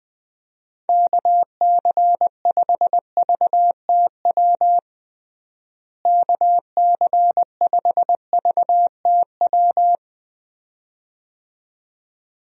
Morse code, heard as a keyed tone, about 20 wpm, 700 Hz. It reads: KC5VTW KC5VTW
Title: USA Call Signs - 6 Characters - Repeated 2x - 20wpm